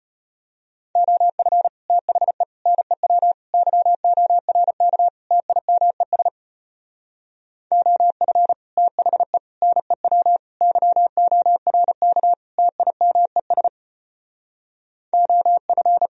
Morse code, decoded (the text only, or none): OF THE NEW YORK TIMES OF THE NEW YORK TIMES OF